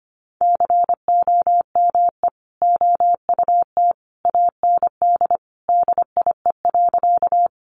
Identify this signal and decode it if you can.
Morse code, heard as a keyed tone, about 25 wpm, 700 Hz. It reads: COME OUT AND DIE.
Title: Sentences from Top 1300 Words 25wpm